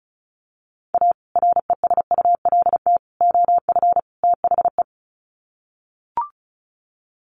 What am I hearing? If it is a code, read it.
Morse code, heard as a keyed tone, about 34 wpm, 700 Hz. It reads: A RESULT OF THE
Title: Sets of 4 Words 35wpm